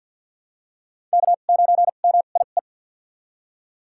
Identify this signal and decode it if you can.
Morse code, heard as a keyed tone, about 53 wpm, spaced as for 37 wpm, 690 Hz. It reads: K9MIE
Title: USA Call Signs - 2x Character Spacing - 50wpm